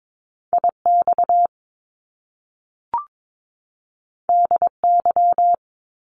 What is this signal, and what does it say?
Morse code, heard as a keyed tone, about 22 wpm, 700 Hz. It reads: IX DY
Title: Top Two Letter Combinations in Words 22wpm